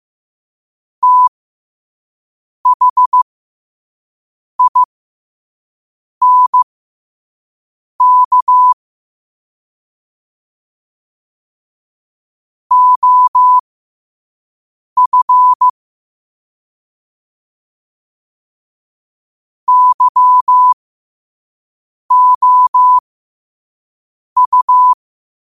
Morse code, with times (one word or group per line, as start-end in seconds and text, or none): 1.0-8.7 THINK
12.7-15.7 OF
19.7-24.9 YOU